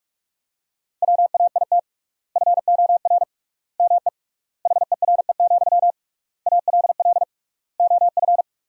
Morse code, heard as a keyed tone, about 48 wpm, 700 Hz. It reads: WAIT FOR ME HERE, ALL OF
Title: 3x Word-Spacing - Sentences from Top 500 Words - 45wpm